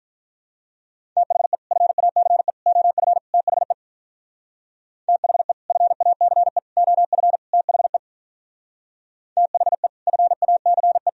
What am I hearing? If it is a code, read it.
Morse code, heard as a keyed tone, about 44 wpm, 700 Hz. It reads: THE FACE OF THE THE FACE OF THE THE FACE
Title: Speed-Racing - Sets of 4 Words 35wpm